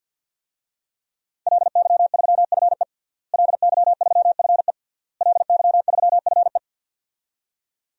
Morse code, heard as a keyed tone, about 53 wpm, 700 Hz. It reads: LY3FE LY3FE LY3FE
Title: DXCC-100 Call Signs - 5 Characters - Repeated 3x - 50wpm